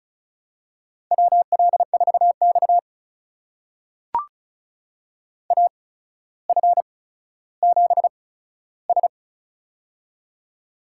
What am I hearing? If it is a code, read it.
Morse code, heard as a keyed tone, about 36 wpm, spaced as for 15 wpm, 700 Hz. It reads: WL4X AF7S
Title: USA Call Signs - 8x Character Spacing - 35wpm